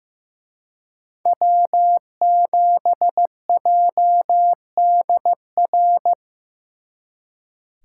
Morse code, heard as a keyed tone, about 18 wpm, 700 Hz. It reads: W7JDR
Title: Speed Racing - USA 1x3 Call Signs 15wpm